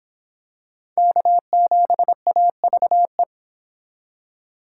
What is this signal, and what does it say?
Morse code, heard as a keyed tone, about 26 wpm, 700 Hz. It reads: K7AVE